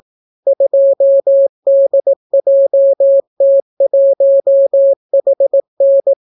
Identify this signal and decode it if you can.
Morse code, heard as a keyed tone, about 18 wpm, 550 Hz. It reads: 2DJT1HN